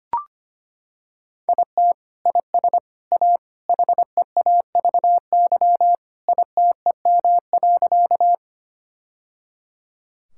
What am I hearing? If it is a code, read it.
Morse code, heard as a keyed tone, about 25 wpm, 700 Hz. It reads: IT IS A HEAVY ITEM.
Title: Sentences from Top 1000 Words - Review Part 6 - 25wpm